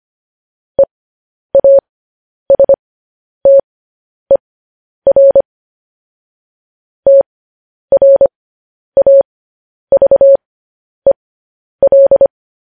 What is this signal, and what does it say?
Morse code, heard as a keyed tone, about 25 wpm, spaced as for 10 wpm, 550 Hz. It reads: EASTER TRAVEL
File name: podcast / News headlines-1x-25x10